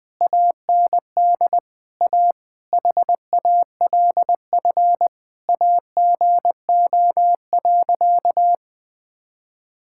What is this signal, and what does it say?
Morse code, heard as a keyed tone, about 20 wpm, 700 Hz. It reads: AND A HALF AGO.
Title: Sentences with 7-Word Phrases - 20wpm